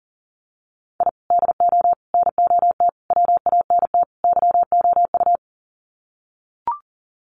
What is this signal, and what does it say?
Morse code, heard as a keyed tone, about 40 wpm, 700 Hz. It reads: I DO NOT WANT YOU
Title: Sets of 5 Words 40wpm